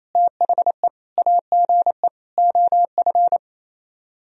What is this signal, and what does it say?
Morse code, heard as a keyed tone, about 35 wpm, 700 Hz. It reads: THE AGE OF
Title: Speed-Racing - Sets of 4 Words 28wpm